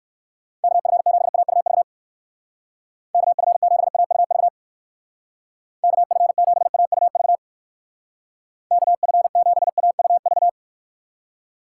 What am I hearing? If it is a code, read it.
Morse code, heard as a keyed tone, about 56 wpm, 700 Hz. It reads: KF7AUV KF7AUV KF7AUV KF7AUV
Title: Speed Racing - USA 2x3 Call Signs 45wpm